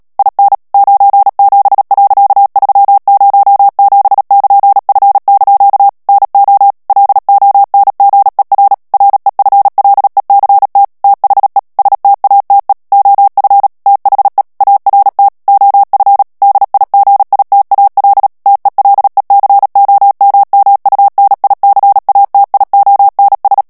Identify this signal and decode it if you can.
Morse code, heard as a keyed tone, about 37 wpm, 800 Hz. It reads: IN 97.307<KN>F) NO LONGER REFLECT THE STATE OF THE ART OF DIGITAL TELECOMMUNICATIONS
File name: morse mp3 / ANGLAIS texte clair 3 vitesse 35 mots minute